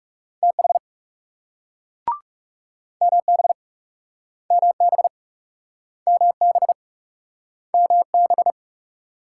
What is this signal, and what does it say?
Morse code, heard as a keyed tone, about 37 wpm, 700 Hz. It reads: TH MB MB MB MB
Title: Speed-Racing Multi-letter Phonemes - 30wpm